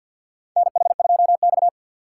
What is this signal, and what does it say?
Morse code, heard as a keyed tone, about 53 wpm, 700 Hz. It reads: NH2X
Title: USA Call Signs - 4 Characters - 3x Character Spacing - 50wpm